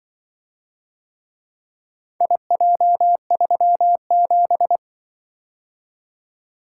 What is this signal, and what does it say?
Morse code, heard as a keyed tone, about 24 wpm, 700 Hz. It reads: IJ37
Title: Speed-Racing - Four Letter-Number Combinations - 20wpm